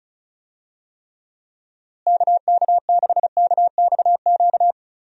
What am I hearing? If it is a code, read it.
Morse code, heard as a keyed tone, about 36 wpm, 700 Hz. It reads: KK6KXQ
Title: USA Call Signs - 6 Characters - Repeated 2x - 35wpm